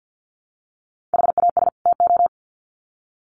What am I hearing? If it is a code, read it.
Morse code, heard as a keyed tone, about 50 wpm, 700 Hz. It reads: HAS TO
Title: Sets of 2 Words 50wpm